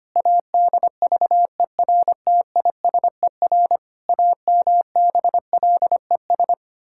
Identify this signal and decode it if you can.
Morse code, heard as a keyed tone, about 25 wpm, 700 Hz. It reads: ADVERTISER AMBLES
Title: Pangram Challenge v2 - 25wpm